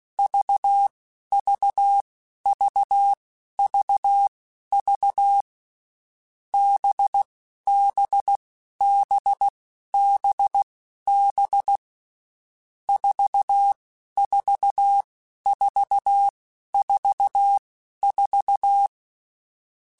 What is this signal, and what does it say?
Morse code, heard as a keyed tone, about 16 wpm, spaced as for 11 wpm, 800 Hz. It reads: VVVVV BBBBB 44444
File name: Groups@10 / Groups@10 mp3s / VB46@10